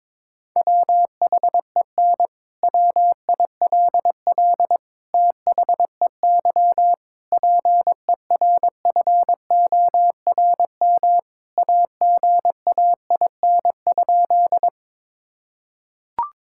Morse code, heard as a keyed tone, about 22 wpm, 700 Hz. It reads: WHEN WILL THEY PERFORM AGAIN?
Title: Sentences from Top 1000 Words - Review Part 5 - 22wpm